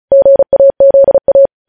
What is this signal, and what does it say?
Morse code, heard as a keyed tone, about 36 wpm, 550 Hz. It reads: GAZA
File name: podcast / News headlines-1x-35x35